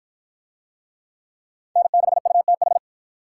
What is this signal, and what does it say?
Morse code, heard as a keyed tone, about 50 wpm, 700 Hz. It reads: N6UTH